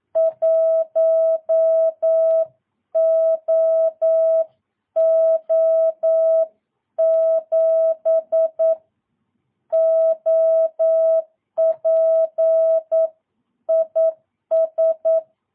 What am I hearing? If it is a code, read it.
Morse code, heard as a keyed tone, about 9 wpm, 650 Hz. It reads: 1OO7 OP IS
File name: jh8cbh / 2022cw